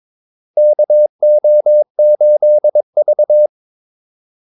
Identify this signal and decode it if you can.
Morse code, heard as a keyed tone, about 22 wpm, 600 Hz. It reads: KO8V
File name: Callsigns-22wpm